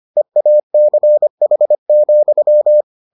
Morse code, heard as a keyed tone, about 25 wpm, 600 Hz. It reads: EACH,